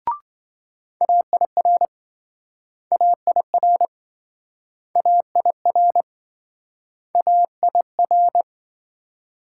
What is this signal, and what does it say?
Morse code, heard as a keyed tone, about 27 wpm, 700 Hz. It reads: AIR AIR AIR AIR